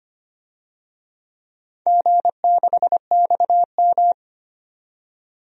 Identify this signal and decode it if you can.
Morse code, heard as a keyed tone, about 25 wpm, 700 Hz. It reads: G6XM
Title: DXCC-100 Call Signs - 4 Characters - 25wpm